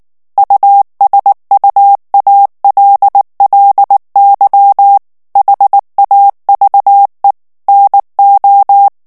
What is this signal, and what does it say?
Morse code, heard as a keyed tone, about 19 wpm, 800 Hz. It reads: USUALLY HAVE NO
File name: morse mp3 / ANGLAIS texte clair 2 vitesse 18 mots minute